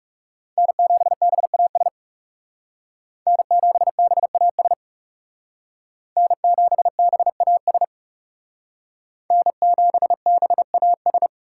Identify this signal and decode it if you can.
Morse code, heard as a keyed tone, about 36 wpm, 700 Hz. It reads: N7BAS N7BAS N7BAS N7BAS
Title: Speed Racing - USA 1x3 Call Signs 30wpm